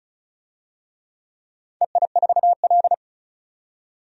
Morse code, heard as a keyed tone, about 36 wpm, 700 Hz. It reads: EI4L
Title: DXCC-100 Call Signs - 4 Characters - 35wpm